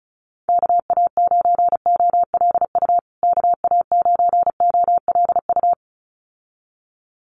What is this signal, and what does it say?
Morse code, heard as a keyed tone, about 34 wpm, 700 Hz. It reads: KA9OLU KA9OLU
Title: USA Call Signs Repeated 2x 35wpm